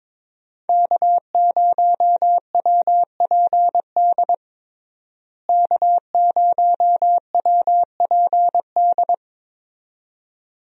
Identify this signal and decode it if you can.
Morse code, heard as a keyed tone, about 22 wpm, 700 Hz. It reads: K0WPD K0WPD
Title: USA Call Signs - 5 Characters - Repeated 2x - 22wpm